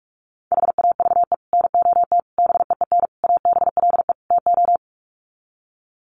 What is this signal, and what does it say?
Morse code, heard as a keyed tone, about 44 wpm, 700 Hz. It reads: HAVE NOT BEEN ABLE TO
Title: Sets of 5 Words 45wpm